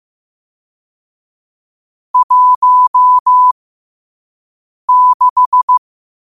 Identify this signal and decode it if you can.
Morse code, heard as a keyed tone, about 15 wpm, spaced as for 5 wpm, 1000 Hz. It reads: 16
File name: qotdfiles / 5wpm